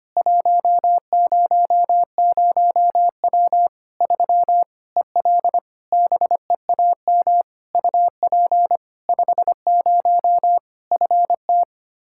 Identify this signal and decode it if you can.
Morse code, heard as a keyed tone, about 25 wpm, 700 Hz. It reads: 100W 3 EL BEAM UP 50 FT
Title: K1USN SST QSO Practice 25wpm